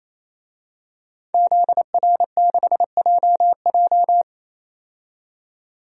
Morse code, heard as a keyed tone, about 28 wpm, 700 Hz. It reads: ZR6JJ
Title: DXCC-100 Call Signs - 5 Characters - 28wpm